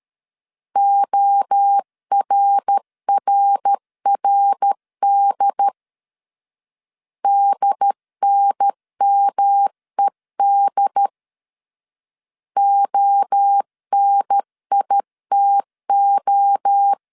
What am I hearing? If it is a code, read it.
Morse code, heard as a keyed tone, about 13 wpm, 800 Hz. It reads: ORRRD DNMED ONITO
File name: CW / DET / 1 3 1500C S A 12wpm 2905